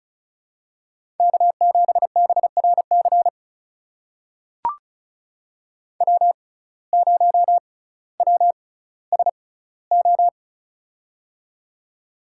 Morse code, heard as a keyed tone, about 36 wpm, spaced as for 18 wpm, 700 Hz. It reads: K7BRC W0WSO
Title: USA Call Signs - 5 Characters - 6x Character Spacing - 35wpm